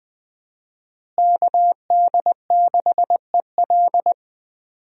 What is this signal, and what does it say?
Morse code, heard as a keyed tone, about 21 wpm, 700 Hz. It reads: KD6EL
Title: USA Call Signs - 5 Characters - Repeated 2x - 20wpm